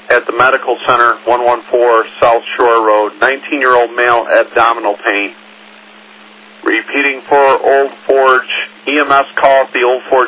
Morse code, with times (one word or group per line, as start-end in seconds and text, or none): none